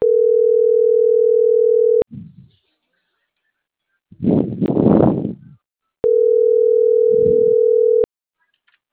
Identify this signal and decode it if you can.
Ringback tone (North America), 440 + 480 Hz.